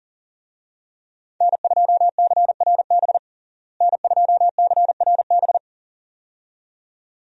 Morse code, heard as a keyed tone, about 37 wpm, 700 Hz. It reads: N2CRB N2CRB
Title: USA Call Signs - 5 Characters - Repeated 2x - 40wpm